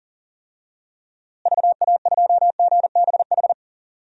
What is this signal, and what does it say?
Morse code, heard as a keyed tone, about 40 wpm, 700 Hz. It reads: VA2GBH
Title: DXCC-100 Call Signs - 6 Characters - Repeated 3x - 40wpm